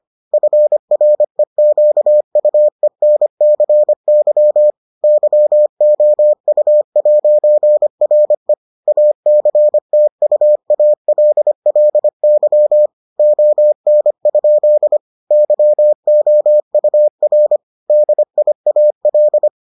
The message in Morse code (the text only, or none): FREQUENCY YOU'RE ACTUALLY ON? YOUR DIAL